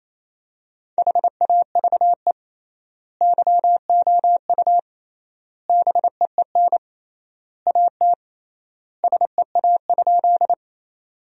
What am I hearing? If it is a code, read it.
Morse code, heard as a keyed tone, about 27 wpm, 700 Hz. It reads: HAVE YOU BEEN AT SEA?
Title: 3x Word-Spacing - Sentences from Top 300 Words - 28wpm